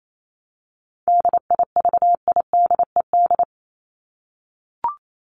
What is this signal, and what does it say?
Morse code, heard as a keyed tone, about 28 wpm, 700 Hz. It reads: DIVIDED
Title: ED Suffix - Top 100 Words with Base Words 28wpm